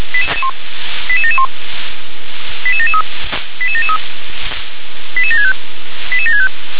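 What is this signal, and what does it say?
Five-tone selective call (ZVEI), heard as quick runs of five tones, tones of about 70 ms. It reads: 88771 88773 88773 88765 88765
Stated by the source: ZVEI five-tone selective call received on SDR.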